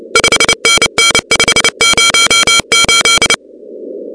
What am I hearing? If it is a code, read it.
Morse code, heard as a keyed tone, about 29 wpm, 2300 Hz, with noise. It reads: 5NN 508
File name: contests / 2019-cw-wpx-kt5j-k5pi- -2330 0-13